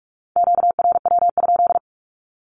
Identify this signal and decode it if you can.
Morse code, heard as a keyed tone, about 44 wpm, 700 Hz. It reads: QRW?
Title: Q-Codes 45wpm